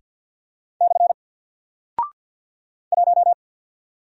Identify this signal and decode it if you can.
Morse code, heard as a keyed tone, about 48 wpm, 700 Hz. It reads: / 1
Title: Single Character (Letter:Number:Punctuation:Prosign) 50wpm